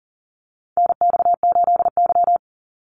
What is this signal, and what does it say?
Morse code, heard as a keyed tone, about 40 wpm, 700 Hz. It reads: NX8Y
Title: Call Signs Repeated 3x - 40wpm